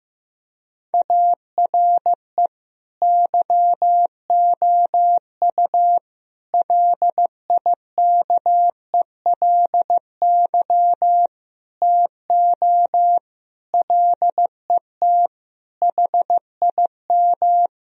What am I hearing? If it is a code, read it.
Morse code, heard as a keyed tone, about 16 wpm, 700 Hz. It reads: ARE YOU LIKELY TO LET HIM
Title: Sentences from Top 1000 Words - Review Part 4 - 15wpm